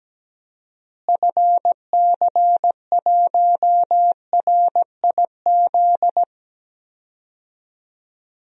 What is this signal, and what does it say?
Morse code, heard as a keyed tone, about 17 wpm, 700 Hz. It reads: FC1RIZ
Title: DXCC-100 Call Signs - 6 Characters - Repeated 2x - 17wpm